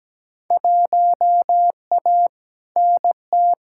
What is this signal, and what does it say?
Morse code, heard as a keyed tone, about 17 wpm, 700 Hz. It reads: 1A NT
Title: Field Day Exchanges Encore - 17wpm